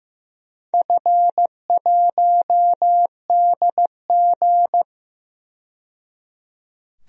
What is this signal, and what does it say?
Morse code, heard as a keyed tone, about 15 wpm, 700 Hz. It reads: F1DG
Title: DXCC-100 Call Signs - 4 Characters - 15wpm